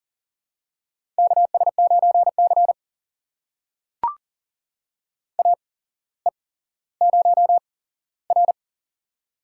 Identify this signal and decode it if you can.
Morse code, heard as a keyed tone, about 38 wpm, spaced as for 17 wpm, 700 Hz. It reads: KS9C AE0R
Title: USA Call Signs - 4 Characters - 8x Character Spacing - 40wpm